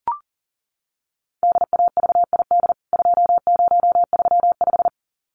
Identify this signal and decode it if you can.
Morse code, heard as a keyed tone, about 40 wpm, 700 Hz. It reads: DAVID 2035
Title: CWT Exchange 40wpm